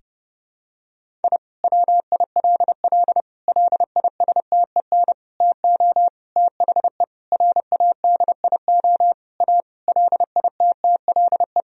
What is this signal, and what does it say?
Morse code, heard as a keyed tone, about 30 wpm, 700 Hz. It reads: I WILL LISTEN TO THE RADIO A LITTLE
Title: Sentences from Top 1000 Words - Review Part 3 - 30wpm